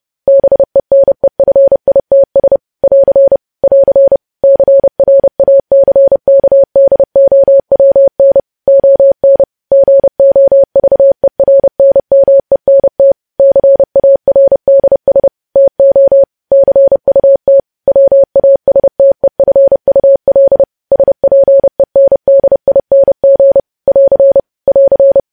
Morse code, heard as a keyed tone, about 30 wpm, 550 Hz. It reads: BENEFITS <AR> <AR> CRACKDOWN ON GOVERNMENT CARDS TO CUT WASTEFUL SPENDING <AR> <AR>
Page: news headlines in morse code at 30 wpm